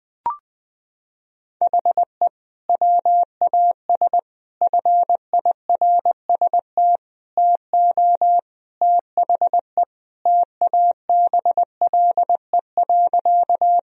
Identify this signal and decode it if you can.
Morse code, heard as a keyed tone, about 20 wpm, 700 Hz. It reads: HE WAS FIRST TO THE TABLE.
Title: Sentences from Top 1000 Words - Review Part 2 - 20wpm